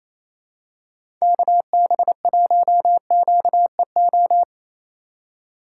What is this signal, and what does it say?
Morse code, heard as a keyed tone, about 28 wpm, 700 Hz. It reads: KB1QEO
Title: USA Call Signs - 6 Characters - 4x Character Spacing - 28wpm